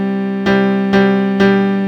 There is no code.